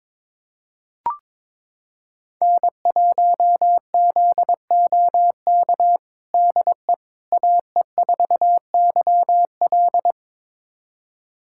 Morse code, heard as a keyed tone, about 22 wpm, 700 Hz. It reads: N1ZOK DE AE4YL
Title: USA Callsign DE Callsign - 5 Characters - 22wpm